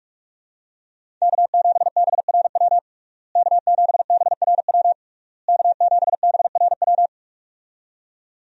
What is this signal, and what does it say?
Morse code, heard as a keyed tone, about 46 wpm, 700 Hz. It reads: K7BRW K7BRW K7BRW
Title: USA Call Signs - 5 Characters - Repeated 3x - 45wpm